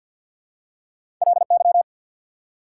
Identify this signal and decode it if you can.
Morse code, heard as a keyed tone, about 54 wpm, 690 Hz. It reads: LY
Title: Prefixes and Suffixes 50wpm